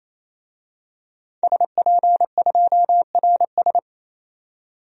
Morse code, heard as a keyed tone, about 28 wpm, 700 Hz. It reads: SP2RS
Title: DXCC-100 Call Signs - 5 Characters - 28wpm